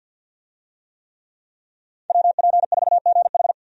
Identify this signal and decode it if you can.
Morse code, heard as a keyed tone, about 47 wpm, 700 Hz. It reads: WP4GH
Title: USA Call Signs - 5 Characters - 50wpm